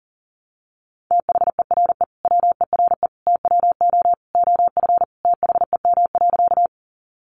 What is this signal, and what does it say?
Morse code, heard as a keyed tone, about 40 wpm, 700 Hz. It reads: THERE WERE TWO OF THEM.
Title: Sentences from Top 100 Words 40wpm